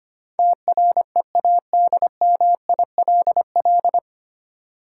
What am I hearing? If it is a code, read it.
Morse code, heard as a keyed tone, about 25 wpm, 700 Hz. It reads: TREADMILL
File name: practice / 25wpm / Difficult Words - Speed-Racing with Hints - 25wpm